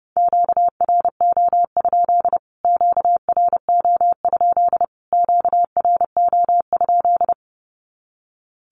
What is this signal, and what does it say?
Morse code, heard as a keyed tone, about 30 wpm, 700 Hz. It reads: QRO? QRO? QRO?
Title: Q-Codes Repeated 3x 30wpm